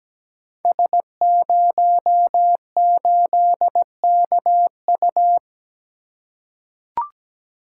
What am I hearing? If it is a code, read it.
Morse code, heard as a keyed tone, about 17 wpm, 700 Hz. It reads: S08KU
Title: Code Groups - 5 Letter:Numbers - 17wpm